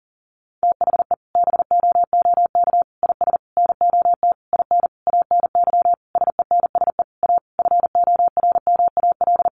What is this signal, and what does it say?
Morse code, heard as a keyed tone, about 40 wpm, 700 Hz. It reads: THE BOOK IS NOT IN ANY SENSE A FORMAL
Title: Sentences from Top 1300 Words 40wpm